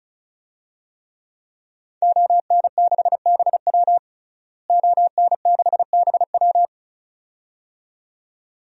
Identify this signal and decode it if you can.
Morse code, heard as a keyed tone, about 36 wpm, 700 Hz. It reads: ON6BW ON6BW
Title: DXCC-100 Call Signs - 5 Characters - Repeated 2x - 35wpm